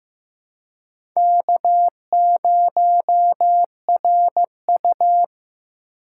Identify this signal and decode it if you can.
Morse code, heard as a keyed tone, about 15 wpm, 700 Hz. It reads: K0RU
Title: USA Call Signs - 4x Character Spacing - 15wpm